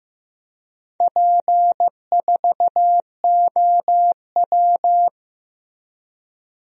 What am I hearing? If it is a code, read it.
Morse code, heard as a keyed tone, about 15 wpm, 700 Hz. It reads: P4OW
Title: DXCC-100 Call Signs - 4 Characters - Repeated 2x - 15wpm